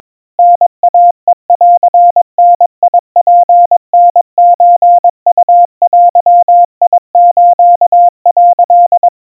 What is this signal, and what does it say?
Morse code (Japanese Wabun code), heard as a keyed tone, about 22 wpm, 700 Hz. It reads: タイヘンダツタソウデス」
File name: 80 cpm 700 Hz Crispy